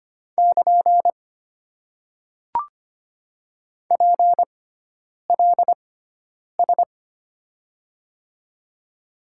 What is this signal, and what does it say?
Morse code, heard as a keyed tone, about 25 wpm, spaced as for 9 wpm, 700 Hz. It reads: <KN> PLS
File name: Top 90 QSO Elements - 6x Character Spacing - 25wpm